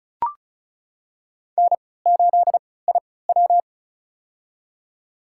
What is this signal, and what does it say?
Morse code, heard as a keyed tone, about 34 wpm, spaced as for 20 wpm, 700 Hz. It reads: N8IW
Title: USA Call Signs - 4 Characters - 3x Character Spacing - 35wpm